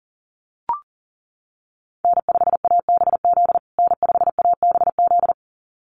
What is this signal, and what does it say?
Morse code, heard as a keyed tone, about 40 wpm, 700 Hz. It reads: N5ABZ N5ABZ
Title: USA Call Signs Repeated 2x 40wpm